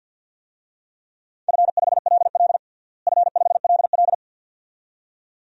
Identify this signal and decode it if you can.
Morse code, heard as a keyed tone, about 49 wpm, 700 Hz. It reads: F5LL F5LL
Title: DXCC-100 Call Signs - 4 Characters - Repeated 2x - 50wpm